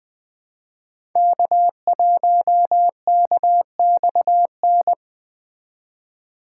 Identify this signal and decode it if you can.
Morse code, heard as a keyed tone, about 20 wpm, 700 Hz. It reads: K1KXN
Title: Speed Racing - USA 1x3 Call Signs 17wpm